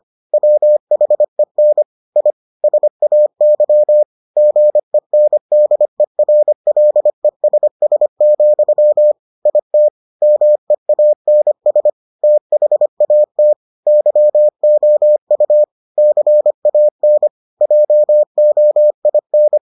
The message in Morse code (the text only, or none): WHEN I SAY GENDERLESS, IT MEANS THAT YOU CAN JOIN